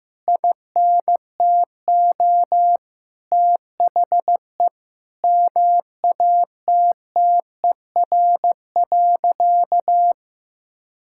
Morse code, heard as a keyed tone, about 16 wpm, 700 Hz. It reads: INTO THE MATTER.
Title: Sentences from Top 1000 Words - Review Part 6 - 15wpm